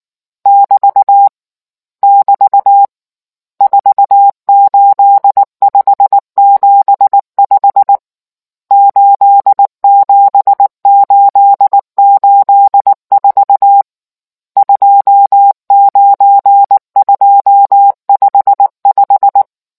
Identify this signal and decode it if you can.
Morse code, heard as a keyed tone, about 19 wpm, 800 Hz. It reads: <BT> <BT> 48575 87884 29255